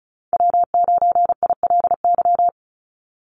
Morse code, heard as a keyed tone, about 36 wpm, 700 Hz. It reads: W9ILY
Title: Call Signs - 35wpm